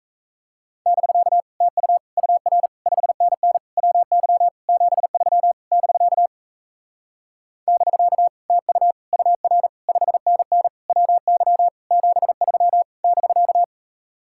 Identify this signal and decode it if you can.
Morse code, heard as a keyed tone, about 38 wpm, 700 Hz. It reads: <BK> TU UR 5NN WY 73 <BK> <BK> TU UR 5NN WY 73 <BK>